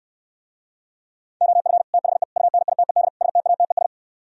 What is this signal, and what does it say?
Morse code, heard as a keyed tone, about 67 wpm, 700 Hz. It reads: OF THE UNITED STATES